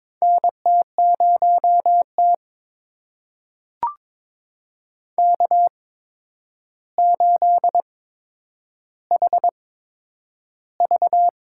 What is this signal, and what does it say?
Morse code, heard as a keyed tone, about 22 wpm, spaced as for 9 wpm, 700 Hz. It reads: NT0T K8HV
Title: USA Call Signs - 4 Characters - 8x Character Spacing - 22wpm